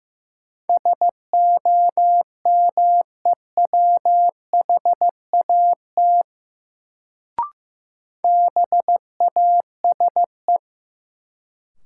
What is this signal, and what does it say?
Morse code, heard as a keyed tone, about 15 wpm, 700 Hz. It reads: SOMEWHAT BASE
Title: Sentences from Top 1000 Words - Review Part 1 - New Word Drill - 15wpm